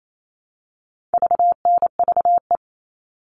Morse code, heard as a keyed tone, about 28 wpm, 700 Hz. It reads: VNVE
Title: All Cut Numbers - 4-Digit 28wpm